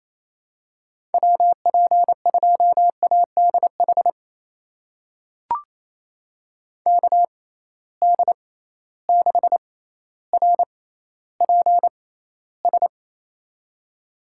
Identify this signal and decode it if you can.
Morse code, heard as a keyed tone, about 27 wpm, spaced as for 14 wpm, 700 Hz. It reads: WP2ADH KD6RPS
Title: USA Call Signs - 6 Characters - 6x Character Spacing - 28wpm